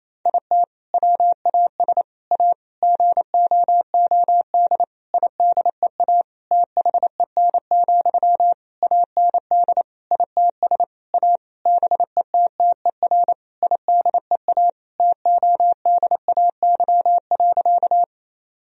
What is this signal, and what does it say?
Morse code, heard as a keyed tone, about 28 wpm, 700 Hz. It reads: IT WAS A GOOD IDEA THEN, AND ITS A BETTER IDEA TODAY.
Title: Sentences from Top 1000 Words - Review Part 4 - 28wpm